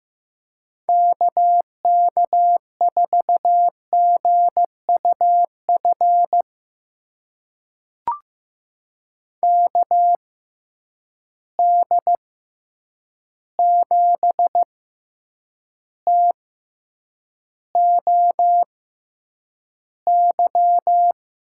Morse code, heard as a keyed tone, about 15 wpm, spaced as for 8 wpm, 700 Hz. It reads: KK4GUF KD7TOY
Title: USA Call Signs - 6 Characters - 6x Character Spacing - 15wpm